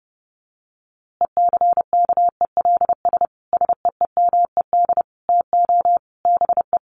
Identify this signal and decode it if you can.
Morse code, heard as a keyed tone, about 30 wpm, 700 Hz. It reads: ECKELS SEEMED TO BE
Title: A Sound Of Thunder 30wpm